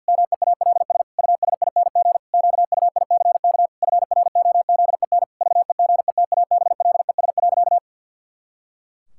Morse code, heard as a keyed tone, about 49 wpm, 700 Hz. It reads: MEALS USING QUICK FROZEN VEGETABLES.
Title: Pangram Challenge v2 - 50wpm